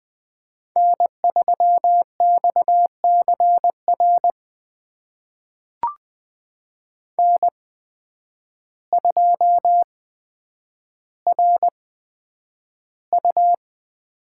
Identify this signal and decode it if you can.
Morse code, heard as a keyed tone, about 20 wpm, spaced as for 9 wpm, 700 Hz. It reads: N3XCR N2RU
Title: USA Call Signs - 5 Characters - 8x Character Spacing - 20wpm